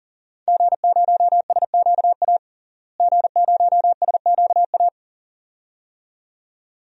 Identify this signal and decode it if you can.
Morse code, heard as a keyed tone, about 40 wpm, 700 Hz. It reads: G0SQA G0SQA
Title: DXCC-100 Call Signs - 5 Characters - Repeated 2x - 40wpm